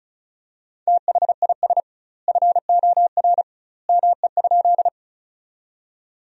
Morse code, heard as a keyed tone, about 36 wpm, 700 Hz. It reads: THIS FOR ME?
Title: 2x Word-Spacing - Sentences from Top 200 Words - 35wpm